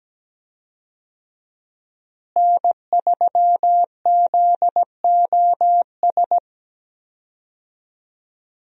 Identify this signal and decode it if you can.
Morse code, heard as a keyed tone, about 17 wpm, 700 Hz. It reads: N3ZOS
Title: USA Call Signs - 5 Characters - Repeated 2x - 17wpm